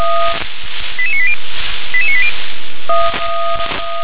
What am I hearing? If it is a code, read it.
Five-tone selective call (ZVEI), heard as quick runs of five tones, tones of about 70 ms. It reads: 88980 88980